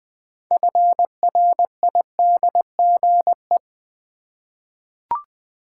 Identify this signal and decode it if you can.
Morse code, heard as a keyed tone, about 20 wpm, 700 Hz. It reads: FRIDGE
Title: 6 Letter Words - First 3 Letters Given - 20wpm